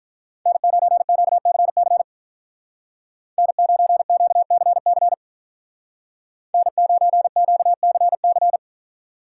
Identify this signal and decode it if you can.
Morse code, heard as a keyed tone, about 47 wpm, 700 Hz. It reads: N9QCC N9QCC N9QCC